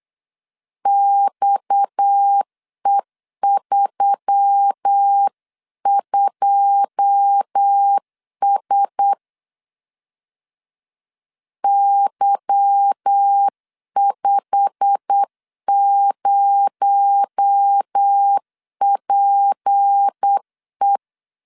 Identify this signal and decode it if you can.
Morse code, heard as a keyed tone, about 8 wpm, 800 Hz. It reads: XE32S Y50PE
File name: CW / DET / 1 9 1500C S C 8wpm 5653